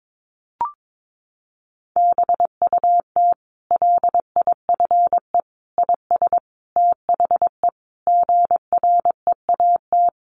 Morse code, heard as a keyed tone, about 22 wpm, 700 Hz. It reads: BUT LIFE IS THE GREAT
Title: Sentences from Top 900 Words 22wpm